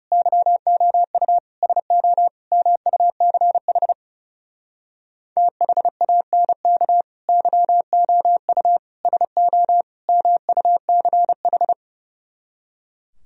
Morse code, heard as a keyed tone, about 38 wpm, 700 Hz. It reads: YOU SO MUCH THANK YOU SO MUCH